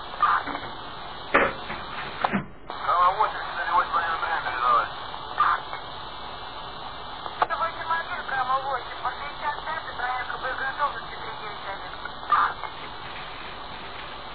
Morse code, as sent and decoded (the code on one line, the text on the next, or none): none